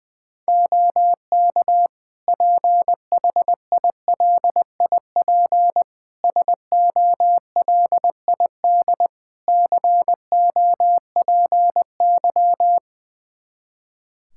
Morse code, heard as a keyed tone, about 20 wpm, 700 Hz. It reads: OK PHILIP SOLID COPY
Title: K1USN SST QSO Practice 20wpm